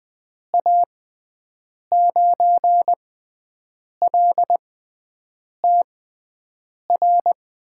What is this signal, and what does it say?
Morse code, heard as a keyed tone, about 20 wpm, spaced as for 7 wpm, 700 Hz. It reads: A9LTR
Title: USA Call Signs - 6 Characters - 6x Character Spacing - 20wpm